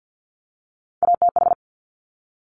Morse code, heard as a keyed tone, about 50 wpm, 700 Hz. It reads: ATH
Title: Top Three Letter Combinations in Words 50wpm